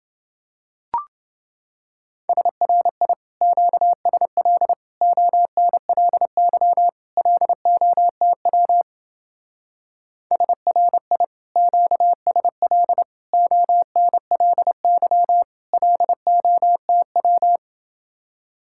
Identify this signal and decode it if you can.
Morse code, heard as a keyed tone, about 28 wpm, 700 Hz. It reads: SRI QSL ONLY LOTW SRI QSL ONLY LOTW